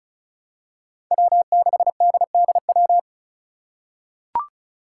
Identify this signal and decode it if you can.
Morse code, heard as a keyed tone, about 36 wpm, 700 Hz. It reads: W6DDW
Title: Five Challenging Letter-Number Combinations 35wpm